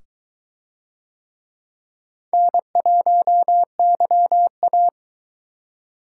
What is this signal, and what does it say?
Morse code, heard as a keyed tone, about 22 wpm, 700 Hz. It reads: N1YA